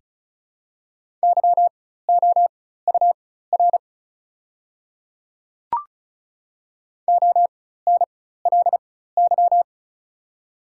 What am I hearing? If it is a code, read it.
Morse code, heard as a keyed tone, about 36 wpm, spaced as for 16 wpm, 700 Hz. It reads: YOUR ONLY
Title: 4x Character Spacing - Top 200 Words - 35wpm